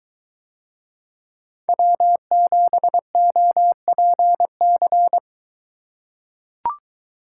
Morse code, heard as a keyed tone, about 23 wpm, 700 Hz. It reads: W7OPC